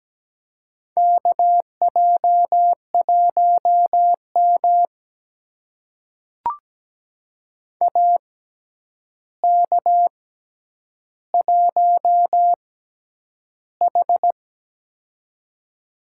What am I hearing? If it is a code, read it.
Morse code, heard as a keyed tone, about 17 wpm, spaced as for 9 wpm, 700 Hz. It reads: KJ1M AK1H
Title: USA Call Signs - 4 Characters - 6x Character Spacing - 17wpm